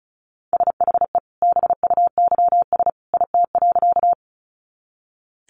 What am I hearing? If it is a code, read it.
Morse code, heard as a keyed tone, about 34 wpm, 700 Hz. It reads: SHE BUYS IT.
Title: Additional Easy Phrases 35wpm